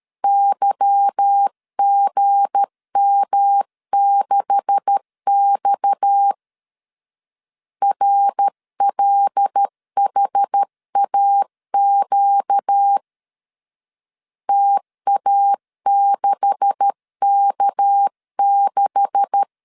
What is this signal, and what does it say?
Morse code, heard as a keyed tone, about 13 wpm, 800 Hz. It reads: YGM6X RLHAQ TA6K6
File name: LEZIONI / 9 286 12wpm 2305